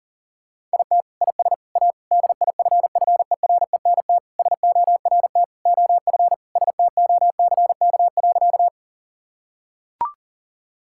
Morse code, heard as a keyed tone, about 40 wpm, 700 Hz. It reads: IT IS A DIFFERENT SORT OF STOCK.
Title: Sentences from Top 1000 Words - Review Part 5 - 40wpm